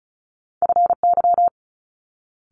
Morse code, heard as a keyed tone, about 36 wpm, 700 Hz. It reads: FY
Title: Multi-letter Phonemes Quick Fire 35wpm